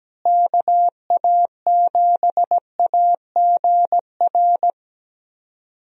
Morse code, heard as a keyed tone, about 17 wpm, 700 Hz. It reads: KA7AGR
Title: USA Call Signs - 6 Characters - Repeated 2x - 17wpm